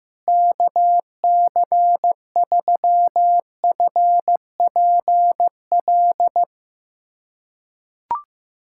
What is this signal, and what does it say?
Morse code, heard as a keyed tone, about 15 wpm, 700 Hz. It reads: KC3FPL
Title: USA Callsign DE Callsign - 6 Characters - 15wpm